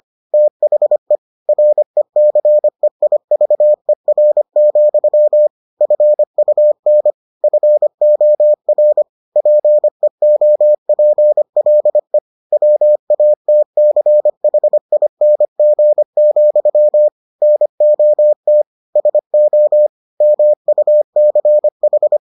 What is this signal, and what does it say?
Morse code, heard as a keyed tone, about 25 wpm, 600 Hz. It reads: THE RECEIVER, FUN FOR PEOPLE WATCHING, NOT SO MUCH